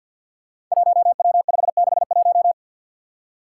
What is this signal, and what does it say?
Morse code, heard as a keyed tone, about 48 wpm, 700 Hz. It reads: 1W561
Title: Five Challenging Letter-Number Combinations 50wpm